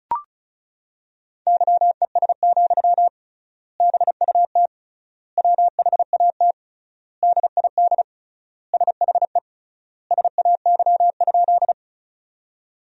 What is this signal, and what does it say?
Morse code, heard as a keyed tone, about 34 wpm, 700 Hz. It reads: YES, BUT WHAT DID SHE SAY?
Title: 3x Word-Spacing - Sentences from Top 500 Words - 35wpm